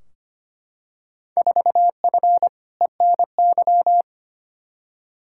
Morse code, heard as a keyed tone, about 25 wpm, 700 Hz. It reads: 4F ENY
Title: Field Day Exchanges - 25wpm